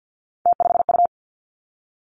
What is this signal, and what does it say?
Morse code, heard as a keyed tone, about 50 wpm, 700 Hz. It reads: T5U
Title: Three Challenging Letter-Number Combinations 50wpm